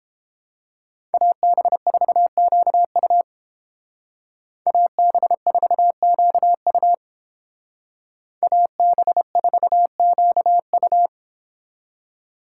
Morse code, heard as a keyed tone, about 29 wpm, 700 Hz. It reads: AB4QU AB4QU AB4QU